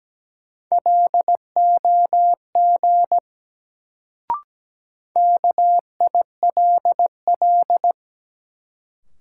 Morse code, heard as a keyed tone, about 17 wpm, 700 Hz. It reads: LOG KILL
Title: Sentences from Top 1400 Words - New Word Drill - 17wpm